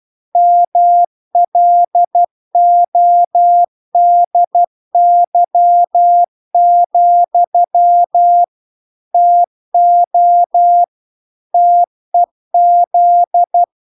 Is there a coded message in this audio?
Yes: Morse code from 0.4 to 13.6 s, heard as a keyed tone, about 12 wpm, 700 Hz. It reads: MLODY, TO TEZ